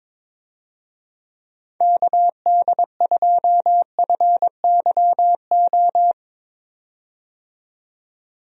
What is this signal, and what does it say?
Morse code, heard as a keyed tone, about 22 wpm, 700 Hz. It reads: KD2FYO
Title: USA Call Signs - 6 Characters - 22wpm